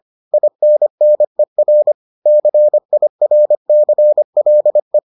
Morse code, heard as a keyed tone, about 25 wpm, 600 Hz. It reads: INNER CIRCLE